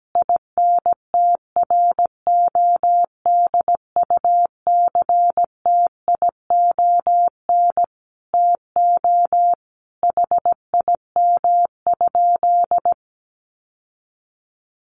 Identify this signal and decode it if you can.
Morse code, heard as a keyed tone, about 17 wpm, 700 Hz. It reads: INTRODUCTION TO HIM?
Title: Sentences from Top 1000 Words 17wpm